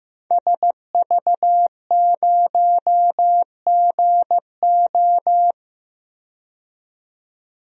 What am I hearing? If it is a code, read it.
Morse code, heard as a keyed tone, about 15 wpm, 700 Hz. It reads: SV0GO
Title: DXCC-100 Call Signs - 5 Characters - 15wpm